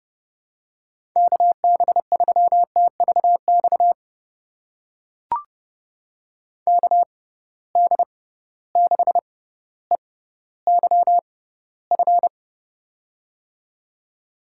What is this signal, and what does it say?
Morse code, heard as a keyed tone, about 32 wpm, spaced as for 16 wpm, 700 Hz. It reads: KB3TVX KD6EYF
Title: USA Call Signs - 6x Character Spacing - 30wpm